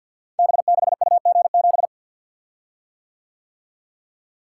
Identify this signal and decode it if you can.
Morse code, heard as a keyed tone, about 52 wpm, 700 Hz. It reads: B6UG7
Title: Code Groups - 5 Letter:Numbers - 50wpm